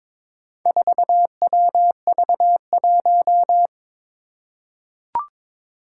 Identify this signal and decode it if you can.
Morse code, heard as a keyed tone, about 22 wpm, 700 Hz. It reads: 4WV1
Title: Four Challenging Letter-Number Combinations 22wpm